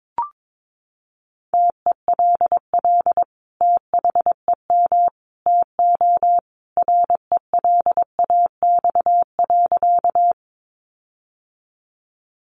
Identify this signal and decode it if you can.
Morse code, heard as a keyed tone, about 22 wpm, 700 Hz. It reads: TELL THEM TO RELAX.
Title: Sentences from Top 1400 Words 22wpm